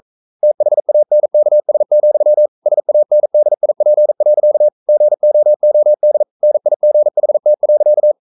Morse code, heard as a keyed tone, about 42 wpm, 600 Hz. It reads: THANKS, SANDIP. GOOD NIGHT.
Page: Romanen Never af Ken Follet på engelsk.